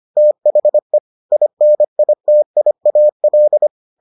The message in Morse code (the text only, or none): THE INITIAL